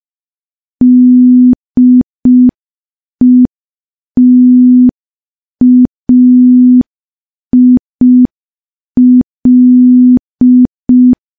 Morse code, heard as a keyed tone, about 5 wpm, 250 Hz. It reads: DETAIL
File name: morse / morse-training-05WPM